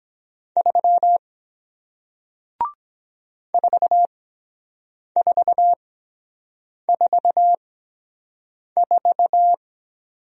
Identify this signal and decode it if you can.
Morse code, heard as a keyed tone, about 23 wpm, spaced as for 7 wpm, 700 Hz. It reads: 3 4444